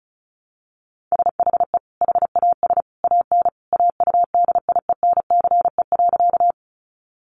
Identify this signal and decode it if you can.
Morse code, heard as a keyed tone, about 34 wpm, 700 Hz. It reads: SHE HAS AN AUDIENCE.
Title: Sentences from Top 800 Words 35wpm